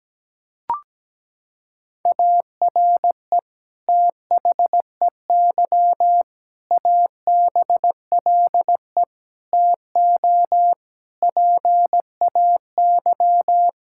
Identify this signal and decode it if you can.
Morse code, heard as a keyed tone, about 17 wpm, 700 Hz. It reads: ARE THEY ABLE TO PAY
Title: Sentences from Top 1000 Words - Review Part 5 - 17wpm